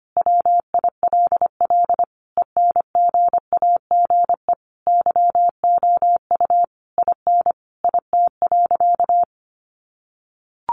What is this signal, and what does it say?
Morse code, heard as a keyed tone, about 25 wpm, 700 Hz. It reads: WILL ENGAGE YOU IN IT.
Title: Sentences from Top 1500 Words 25wpm